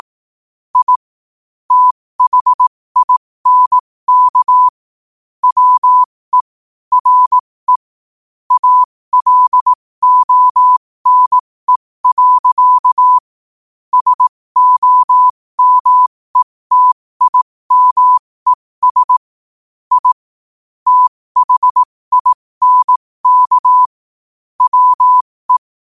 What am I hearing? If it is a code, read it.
Morse code, heard as a keyed tone, about 18 wpm, spaced as for 15 wpm, 1000 Hz. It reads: I THINK WE RE ALONE. SOMETIMES I THINK WE
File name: qotdfiles / 15wpm